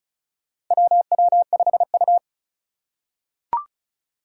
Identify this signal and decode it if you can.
Morse code, heard as a keyed tone, about 36 wpm, 700 Hz. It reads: WW5U
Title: USA Call Signs - 4 Characters - 3x Character Spacing - 35wpm